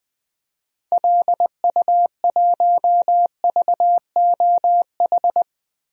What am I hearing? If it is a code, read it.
Morse code, heard as a keyed tone, about 20 wpm, 700 Hz. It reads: LU1VOH
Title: DXCC-100 Call Signs - 6 Characters - 20wpm